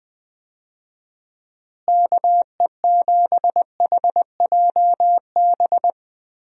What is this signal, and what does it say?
Morse code, heard as a keyed tone, about 20 wpm, 700 Hz. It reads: KE7HJB
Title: USA Call Signs - 6 Characters - Repeated 2x - 20wpm